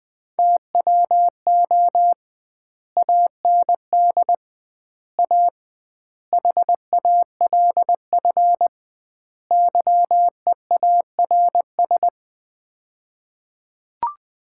Morse code, heard as a keyed tone, about 20 wpm, 700 Hz. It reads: TWO AND A HALF YEARS
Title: 2x Word-Spacing - Sets of 5 Words - 20wpm